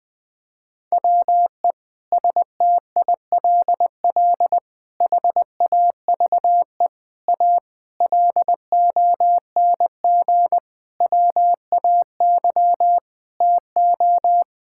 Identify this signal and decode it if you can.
Morse code, heard as a keyed tone, about 20 wpm, 700 Hz. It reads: WE STILL HAVE A LONG WAY TO
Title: Sentences with 5-Word Phrases - 20wpm